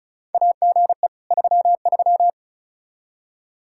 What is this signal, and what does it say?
Morse code, heard as a keyed tone, about 36 wpm, 700 Hz. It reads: AGE 33
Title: Practice QSO - Exchange 3 - More Info - 35wpm